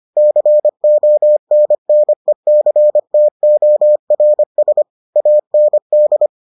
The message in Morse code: CONNECTORS AND

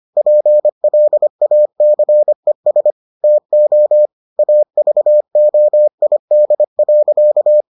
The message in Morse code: PLACES TO AVOID.

